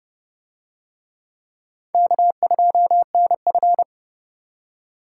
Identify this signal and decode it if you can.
Morse code, heard as a keyed tone, about 27 wpm, 700 Hz. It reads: K2NF